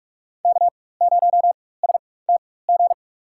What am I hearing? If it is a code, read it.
Morse code, heard as a keyed tone, about 46 wpm, spaced as for 21 wpm, 700 Hz. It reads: K0STG
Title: USA Call Signs - 5 Characters - 4x Character Spacing - 45wpm